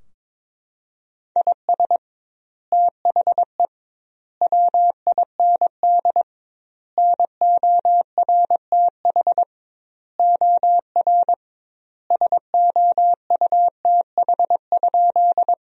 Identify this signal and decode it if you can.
Morse code, heard as a keyed tone, about 22 wpm, 700 Hz. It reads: IS THE WIND NORTH OR SOUTH?
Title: 2x Word-Spacing - Sentences from Top 400 Words - 22wpm